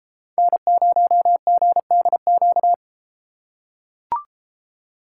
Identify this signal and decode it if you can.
Morse code, heard as a keyed tone, about 33 wpm, 700 Hz. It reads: N0GDQ